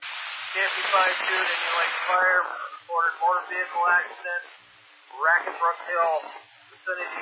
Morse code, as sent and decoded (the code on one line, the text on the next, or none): none